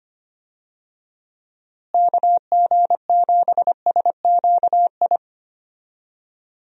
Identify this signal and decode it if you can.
Morse code, heard as a keyed tone, about 25 wpm, 700 Hz. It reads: KG7SQI
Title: USA Call Signs - 6 Characters - Repeated 3x - 25wpm